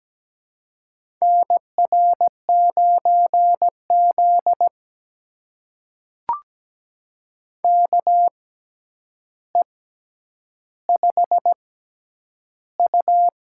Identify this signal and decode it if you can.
Morse code, heard as a keyed tone, about 17 wpm, spaced as for 9 wpm, 700 Hz. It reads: NR9Z KE5U
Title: USA Call Signs - 4 Characters - 6x Character Spacing - 17wpm